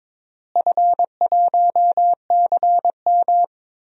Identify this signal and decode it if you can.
Morse code, heard as a keyed tone, about 22 wpm, 700 Hz. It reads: F1CM
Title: DXCC-100 Call Signs - 4 Characters - Repeated 2x - 22wpm